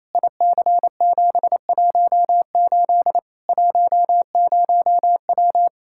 Morse code, heard as a keyed tone, about 28 wpm, 700 Hz. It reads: IC718 10W
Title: Practice QSO - Exchange 2 - Info - 28wpm